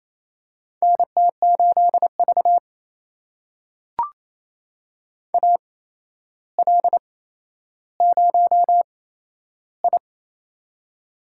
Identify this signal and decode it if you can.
Morse code, heard as a keyed tone, about 28 wpm, spaced as for 12 wpm, 700 Hz. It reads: NT8V AL0I
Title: USA Call Signs - 4 Characters - 8x Character Spacing - 28wpm